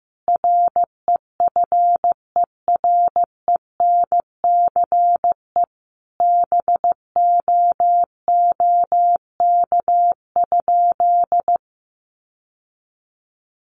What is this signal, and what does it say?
Morse code, heard as a keyed tone, about 15 wpm, 700 Hz. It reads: REFERENCE BOOK?
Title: Sentences from Top 1200 Words 15wpm